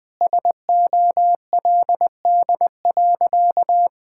Morse code, heard as a keyed tone, about 20 wpm, 700 Hz. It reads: SOLD.
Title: Sentences with 4-Word Phrases - Spoken Beginning - 20wpm